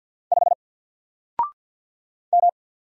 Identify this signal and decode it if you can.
Morse code, heard as a keyed tone, about 50 wpm, 700 Hz. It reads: 5 M
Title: Single Character (Letter:Number:Punctuation:Prosign) 50wpm